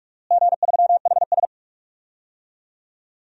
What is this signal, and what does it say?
Morse code, heard as a keyed tone, about 46 wpm, 700 Hz. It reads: G3HS
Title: DXCC-100 Call Signs - 4 Characters - Repeated 3x - 45wpm